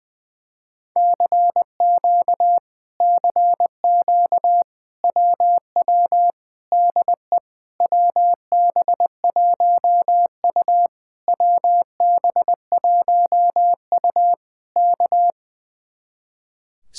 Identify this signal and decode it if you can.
Morse code, heard as a keyed tone, about 20 wpm, 700 Hz. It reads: CQ CQ WW DE WB1U WB1U K
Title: Speed-Racing Common QSO Phrases Encore - 20wpm